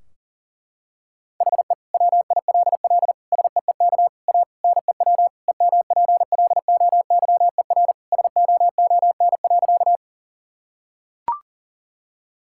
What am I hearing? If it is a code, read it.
Morse code, heard as a keyed tone, about 40 wpm, 700 Hz. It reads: HE WILL SEEK A NEW EMPLOYER SOON.
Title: Sentences from Top 1600 Words 40wpm